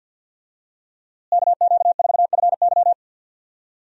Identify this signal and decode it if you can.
Morse code, heard as a keyed tone, about 47 wpm, 700 Hz. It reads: KQ4FY
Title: USA Call Signs - 5 Characters - Repeated 2x - 50wpm